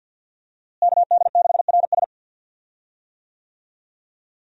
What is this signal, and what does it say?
Morse code, heard as a keyed tone, about 47 wpm, 700 Hz. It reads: KD6RS
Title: USA Call Signs - 5 Characters - Repeated 3x - 50wpm